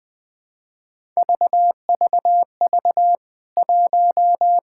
Morse code, heard as a keyed tone, about 21 wpm, 700 Hz. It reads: VVV 1
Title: K1USN SST QSO Practice 20wpm